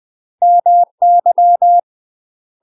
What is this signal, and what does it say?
Morse code, heard as a keyed tone, about 20 wpm, 700 Hz. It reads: MY
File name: UK MASTER QSOs TRAINING / 20WPM / UK MASTER QSOs 0066